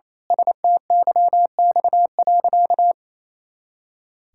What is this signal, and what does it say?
Morse code, heard as a keyed tone, about 28 wpm, 700 Hz. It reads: STYX.
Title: Pangram Challenge v2 - 28wpm